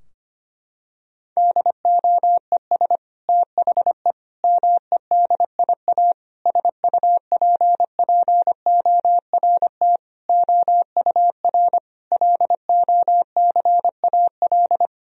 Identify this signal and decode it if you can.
Morse code, heard as a keyed tone, about 25 wpm, 700 Hz. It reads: DOES THE MEDIA SUPPORT OUR LOCAL
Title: Sentences from Top 1000 Words - Review Part 4 - 25wpm